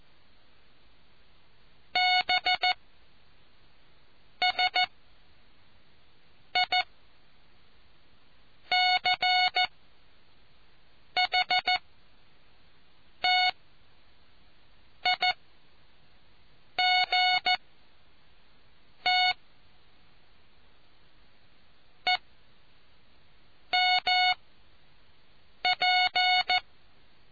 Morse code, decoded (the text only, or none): BSICHTIGTE MP